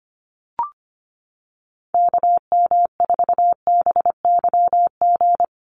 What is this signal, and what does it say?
Morse code, heard as a keyed tone, about 25 wpm, 700 Hz. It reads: KM4BYG
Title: USA Call Signs Repeated 3x 25wpm